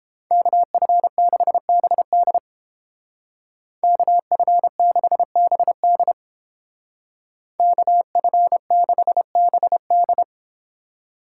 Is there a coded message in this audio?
Yes: Morse code from 0.3 to 10.2 s, heard as a keyed tone, about 29 wpm, 700 Hz. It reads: KF6BD KF6BD KF6BD